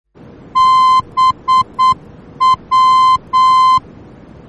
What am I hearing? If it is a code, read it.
Morse code, heard as a keyed tone, about 8 wpm, 1040 Hz. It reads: BW
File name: germany / mx / BW-D